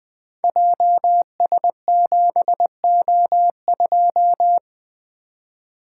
Morse code, heard as a keyed tone, about 20 wpm, 700 Hz. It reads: JS7O2
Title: Code Groups - 5 Letter:Numbers - 20wpm